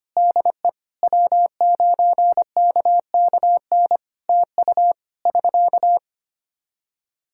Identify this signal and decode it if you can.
Morse code, heard as a keyed tone, about 25 wpm, 700 Hz. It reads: DE W9KKN TU <SK>
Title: Practice QSO - Exchange 4 - Ending - 25wpm